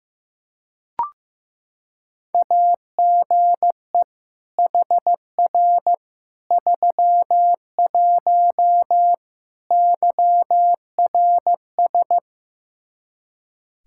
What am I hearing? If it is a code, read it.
Morse code, heard as a keyed tone, about 15 wpm, 700 Hz. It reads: AGE HR 31 YRS
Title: Practice QSO - Exchange 3 - More Info - 15wpm